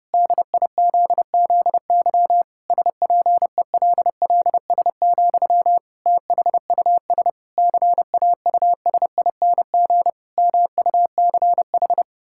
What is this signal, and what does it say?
Morse code, heard as a keyed tone, about 30 wpm, 700 Hz. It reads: DIZZY SPELLS, THUS CAUSING MUCH
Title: Pangram Challenge v2 - 30wpm